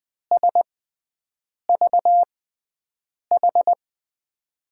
Morse code, heard as a keyed tone, about 20 wpm, spaced as for 7 wpm, 700 Hz. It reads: SVH
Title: USA Call Signs - 6 Characters - 6x Character Spacing - 20wpm